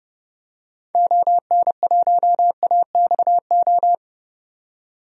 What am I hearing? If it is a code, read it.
Morse code, heard as a keyed tone, about 30 wpm, 700 Hz. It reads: ON1AXO
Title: DXCC-100 Call Signs - 6 Characters - 30wpm